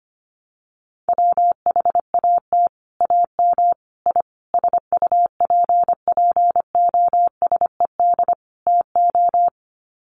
Morse code, heard as a keyed tone, about 25 wpm, 700 Hz. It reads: WHAT AM I SUPPOSED TO
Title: Sets of 5 Words Encore2 25wpm